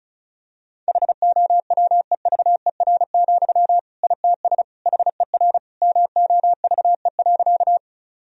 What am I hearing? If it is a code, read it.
Morse code, heard as a keyed tone, about 36 wpm, 700 Hz. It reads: HOWEVER, ITS HER MOVE.
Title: Sentences from Top 1000 Words - Review Part 1 - 35wpm